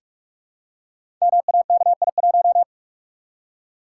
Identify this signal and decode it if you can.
Morse code, heard as a keyed tone, about 44 wpm, 700 Hz. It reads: MAKI1
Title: Code Groups - 5 Letter:Numbers - 45wpm